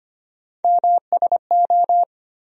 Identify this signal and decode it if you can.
Morse code, heard as a keyed tone, about 25 wpm, 700 Hz. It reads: MSO
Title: USA Call Sign Postfixes - 3 Letters 25wpm